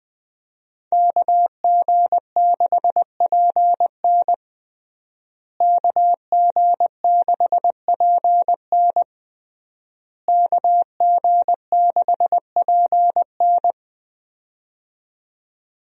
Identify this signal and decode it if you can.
Morse code, heard as a keyed tone, about 20 wpm, 700 Hz. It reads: KG6PN KG6PN KG6PN
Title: USA Call Signs - 5 Characters - Repeated 3x - 20wpm